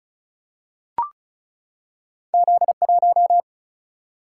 Morse code, heard as a keyed tone, about 34 wpm, 700 Hz. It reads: Z1
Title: Single-Letter Single-Number 35wpm